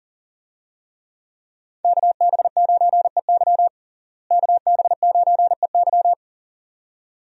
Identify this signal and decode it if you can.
Morse code, heard as a keyed tone, about 40 wpm, 700 Hz. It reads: KB9EY KB9EY
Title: USA Call Signs - 5 Characters - Repeated 2x - 40wpm